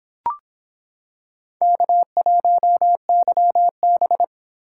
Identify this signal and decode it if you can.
Morse code, heard as a keyed tone, about 25 wpm, 700 Hz. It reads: K1YB